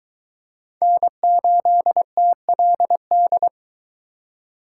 Morse code, heard as a keyed tone, about 23 wpm, 700 Hz. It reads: N8TLD